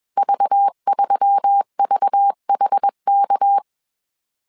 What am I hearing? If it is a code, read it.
Morse code, heard as a keyed tone, about 21 wpm, 800 Hz. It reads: V3VHK
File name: CW / DET / 1 6 1500C S A 20wpm 2048